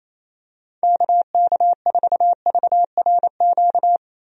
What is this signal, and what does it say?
Morse code, heard as a keyed tone, about 28 wpm, 700 Hz. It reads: KK4VRQ
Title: USA Call Signs - 6 Characters - Repeated 3x - 28wpm